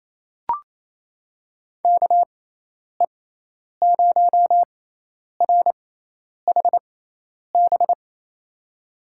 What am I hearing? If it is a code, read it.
Morse code, heard as a keyed tone, about 28 wpm, spaced as for 10 wpm, 700 Hz. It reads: KE0RHB
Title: USA Call Signs - 6 Characters - 6x Character Spacing - 28wpm